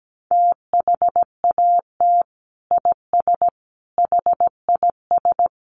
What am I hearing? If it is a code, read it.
Morse code, heard as a keyed tone, about 17 wpm, 700 Hz. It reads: THAT IS HIS
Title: Sentences from Top 1200 Words 17wpm